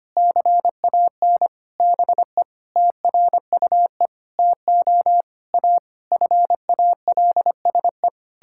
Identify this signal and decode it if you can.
Morse code, heard as a keyed tone, about 25 wpm, 700 Hz. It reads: CAN BE TRUE TO A FALSE
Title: Sentences from Top 1600 Words 25wpm